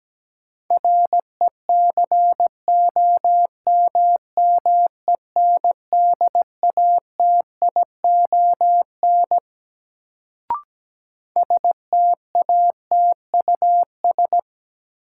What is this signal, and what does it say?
Morse code, heard as a keyed tone, about 17 wpm, 700 Hz. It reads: RECOMMENDATION STATUS
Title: Sentences from Top 1400 Words - New Word Drill Alt - 17wpm